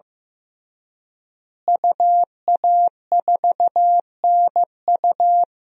Morse code, heard as a keyed tone, about 15 wpm, 700 Hz. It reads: UA4NU
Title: DXCC-100 Call Signs - 5 Characters - Repeated 3x - 15wpm